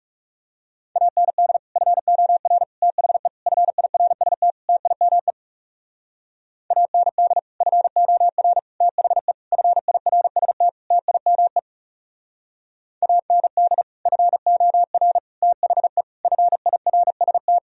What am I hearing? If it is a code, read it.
Morse code, heard as a keyed tone, about 36 wpm, 700 Hz. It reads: AND FOR THE FIRST TIME AND FOR THE FIRST TIME AND FOR THE FIRST
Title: Speed-Racing - Sets of 5 Words 30wpm